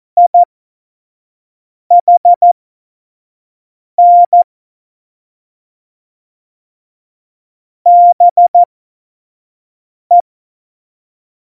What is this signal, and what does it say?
Morse code, heard as a keyed tone, about 14 wpm, spaced as for 5 wpm, 700 Hz. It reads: IHN BE
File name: Morsekurs / Witze / Witz 19. Dezember 2017, 25 F70